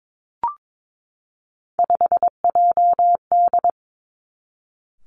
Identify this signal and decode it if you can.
Morse code, heard as a keyed tone, about 22 wpm, 700 Hz. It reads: 5JD
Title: Three Challenging Letter-Number Combinations 22wpm